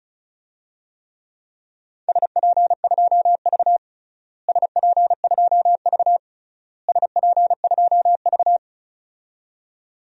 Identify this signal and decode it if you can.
Morse code, heard as a keyed tone, about 36 wpm, 700 Hz. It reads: SP2V SP2V SP2V
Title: DXCC-100 Call Signs - 4 Characters - Repeated 3x - 35wpm